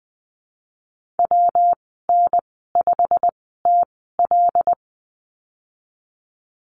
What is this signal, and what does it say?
Morse code, heard as a keyed tone, about 20 wpm, spaced as for 15 wpm, 700 Hz. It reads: WN5TL
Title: 128 Lesson 21g - Number 6 - Call Signs 20wpm